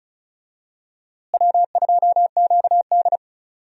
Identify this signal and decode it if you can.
Morse code, heard as a keyed tone, about 36 wpm, 700 Hz. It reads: W2QD
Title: USA Call Signs - 4 Characters - Repeated 3x - 35wpm